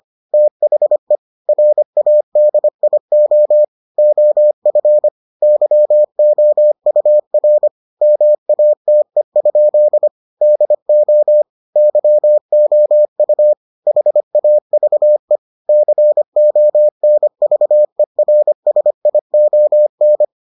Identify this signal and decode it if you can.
Morse code, heard as a keyed tone, about 25 wpm, 600 Hz. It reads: THE RADIO OF YOUR MATE? DO YOU HAVE CONVERSION